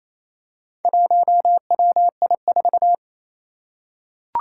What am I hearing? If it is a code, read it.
Morse code, heard as a keyed tone, about 28 wpm, 700 Hz. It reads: 1WI4
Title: Four Challenging Letter-Number Combinations 28wpm